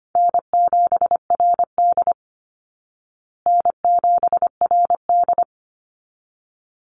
Morse code, heard as a keyed tone, about 25 wpm, 700 Hz. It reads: N7RD N7RD
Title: Easy Call Signs Repeated 2x - 25wpm